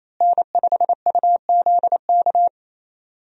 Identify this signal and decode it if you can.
Morse code, heard as a keyed tone, about 28 wpm, 700 Hz. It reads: N5UZK
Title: USA Call Signs - 5 Characters - Repeated 2x - 28wpm